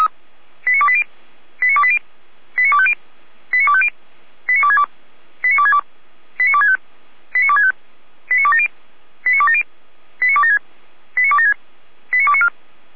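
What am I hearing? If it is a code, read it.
Five-tone selective call (ZVEI), heard as quick runs of five tones, tones of about 70 ms. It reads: 89280 89280 89250 89250 89262 89262 89265 89265 89270 89270 89276 89276 89294
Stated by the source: Five-tone ZVEI selective call received on an SDR.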